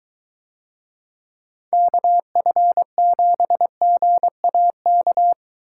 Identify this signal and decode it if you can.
Morse code, heard as a keyed tone, about 23 wpm, 700 Hz. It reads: KF7GAK